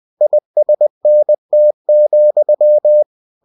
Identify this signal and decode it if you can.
Morse code, heard as a keyed tone, about 20 wpm, 600 Hz. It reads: ISNT,
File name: cw-books / winnie-the-pooh / winnie-20wpm-page-0092